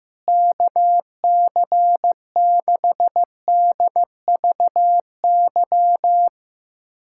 Morse code, heard as a keyed tone, about 16 wpm, 700 Hz. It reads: KC6DVY
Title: USA Callsign DE Callsign 15wpm